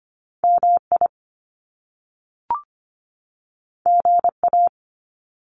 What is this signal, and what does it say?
Morse code, heard as a keyed tone, about 25 wpm, 700 Hz. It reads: MI GA
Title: US States 25wpm